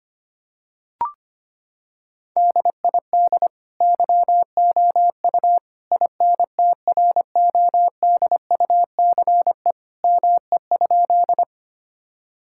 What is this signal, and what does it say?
Morse code, heard as a keyed tone, about 25 wpm, 700 Hz. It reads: DID YOU INTRODUCE ME?
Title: Sentences from Top 1600 Words 25wpm